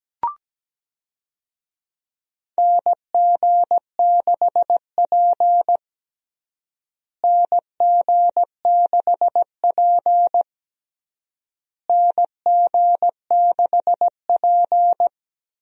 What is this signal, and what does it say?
Morse code, heard as a keyed tone, about 17 wpm, 700 Hz. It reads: NG6P NG6P NG6P
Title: USA Call Signs - 4 Characters - Repeated 3x - 17wpm